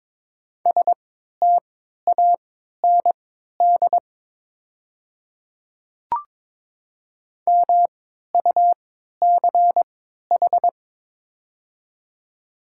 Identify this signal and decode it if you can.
Morse code, heard as a keyed tone, about 22 wpm, spaced as for 13 wpm, 700 Hz. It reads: STAND MUCH
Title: 3x Character Spacing - Top 500 Words - 22wpm